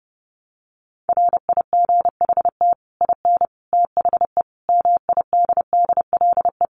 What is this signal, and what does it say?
Morse code, heard as a keyed tone, about 30 wpm, 700 Hz. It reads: RIGHT IN THE MIDDLE
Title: Sets of 5 Words 30wpm